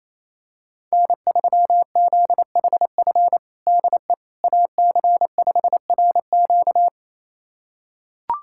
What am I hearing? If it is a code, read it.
Morse code, heard as a keyed tone, about 28 wpm, 700 Hz. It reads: N3ZHF DE AC5RQ
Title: USA Callsign DE Callsign - 5 Characters - 28wpm